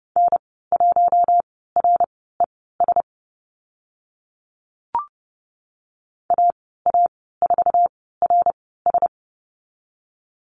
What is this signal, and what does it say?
Morse code, heard as a keyed tone, about 30 wpm, spaced as for 17 wpm, 700 Hz. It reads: N1RES AA4RS
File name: Lesson 14g - Number 2 - Call Signs 30wpm